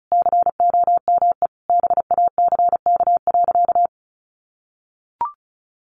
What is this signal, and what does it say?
Morse code, heard as a keyed tone, about 34 wpm, 700 Hz. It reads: COME BACK.
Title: Sentences from Top 1300 Words 35wpm